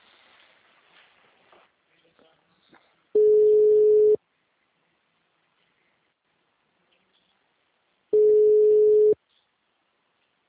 Ringback tone (Europe, 425 Hz).